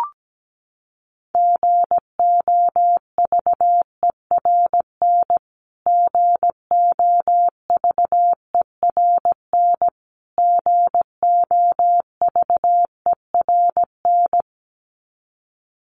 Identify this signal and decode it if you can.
Morse code, heard as a keyed tone, about 17 wpm, 700 Hz. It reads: GOVERN GOVERN GOVERN
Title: Top 500 Words Repeated 17wpm